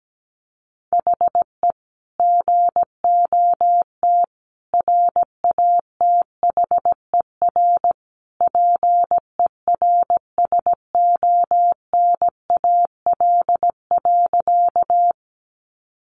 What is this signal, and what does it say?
Morse code, heard as a keyed tone, about 17 wpm, 700 Hz. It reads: HE GOT RATHER PERSONAL.
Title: Sentences from Top 700 Words 17wpm